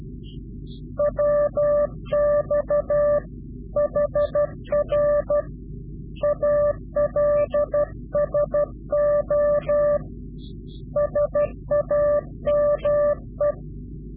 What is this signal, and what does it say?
Morse code, heard as a keyed tone, about 12 wpm, 590 Hz. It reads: WX HR ALSO SAME
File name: CW-QSO / Directional Call / Directional QSO